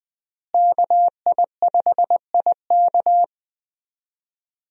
Morse code, heard as a keyed tone, about 20 wpm, 700 Hz. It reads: KI5IK
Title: USA Call Signs - 5 Characters - Repeated 2x - 20wpm